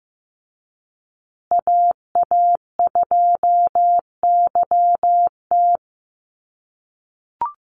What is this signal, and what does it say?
Morse code, heard as a keyed tone, about 15 wpm, 700 Hz. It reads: AA2YT
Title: USA Call Signs Repeated 3x 15wpm